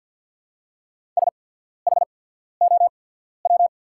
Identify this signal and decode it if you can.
Morse code, heard as a keyed tone, about 48 wpm, spaced as for 14 wpm, 700 Hz. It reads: SHOW
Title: 8x Character Spacing - Top 300 Words - 50wpm